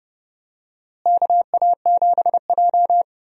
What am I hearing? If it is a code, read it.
Morse code, heard as a keyed tone, about 31 wpm, 700 Hz. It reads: KA7J
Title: USA Call Signs - 4 Characters - Repeated 2x - 30wpm